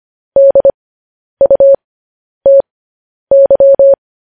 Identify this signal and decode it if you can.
Morse code, heard as a keyed tone, about 25 wpm, spaced as for 10 wpm, 550 Hz. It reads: DUTY